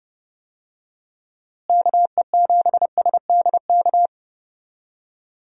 Morse code, heard as a keyed tone, about 30 wpm, 700 Hz. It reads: KE7SDK
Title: USA Call Signs - 6 Characters - Repeated 2x - 30wpm